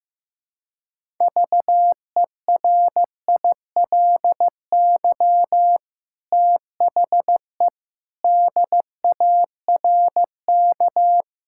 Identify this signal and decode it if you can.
Morse code, heard as a keyed tone, about 15 wpm, 700 Hz. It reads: VERILY THE DARK
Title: Pangram Challenge v2 - 15wpm